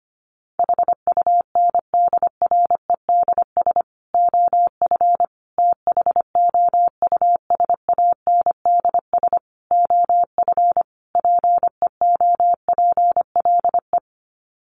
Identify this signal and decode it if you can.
Morse code, heard as a keyed tone, about 25 wpm, 700 Hz. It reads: HUNDREDS OF THOUSANDS OF PEOPLE
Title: Sets of 5 Words Encore2 25wpm